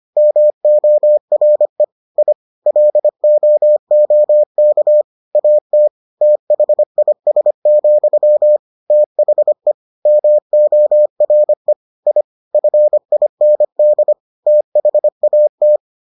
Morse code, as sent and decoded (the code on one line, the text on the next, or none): -- --- .-. / .. / .-.. --- --- -.- / .- - / - .... .. ... --..-- / - .... / -- --- .-. / .. / ..-. .. -. -.. / - .... .- -
MORE I LOOK AT THIS, THE MORE I FIND THAT